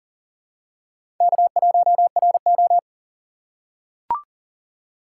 Morse code, heard as a keyed tone, about 39 wpm, 700 Hz. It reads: K1RO
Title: K1USN SST QSO Practice - 40wpm